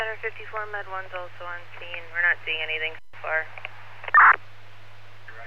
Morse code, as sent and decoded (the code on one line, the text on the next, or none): none